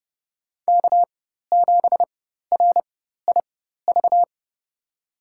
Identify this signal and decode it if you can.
Morse code, heard as a keyed tone, about 31 wpm, spaced as for 14 wpm, 700 Hz. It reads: K7RIV
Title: USA Call Signs - 4x Character Spacing - 30wpm